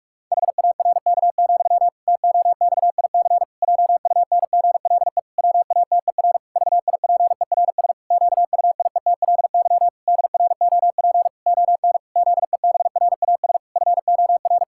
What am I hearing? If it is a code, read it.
Morse code, heard as a keyed tone, about 46 wpm, 700 Hz. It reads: HARK, TOXIC JUNGLE WATER VIPERS QUIETLY DROP ON ZEBRAS FOR
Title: Pangram Challenge v2 - 45wpm